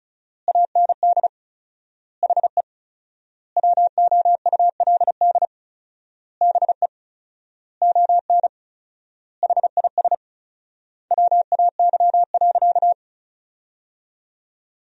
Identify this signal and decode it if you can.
Morse code, heard as a keyed tone, about 34 wpm, 700 Hz. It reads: AND HE WOULD BE ON HIS WAY.
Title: 4x Word-Spacing - Sentences from Top 300 Words - 35wpm